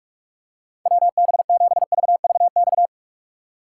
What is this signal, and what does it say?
Morse code, heard as a keyed tone, about 46 wpm, 700 Hz. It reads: WB7VVX